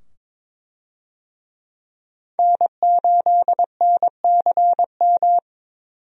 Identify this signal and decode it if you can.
Morse code, heard as a keyed tone, about 22 wpm, 700 Hz. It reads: N8NCM
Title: USA Call Signs - 5 Characters - 22wpm